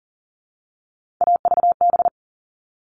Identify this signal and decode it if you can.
Morse code, heard as a keyed tone, about 40 wpm, 700 Hz. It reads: AVB
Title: All Cut Numbers - 3-Digit 40wpm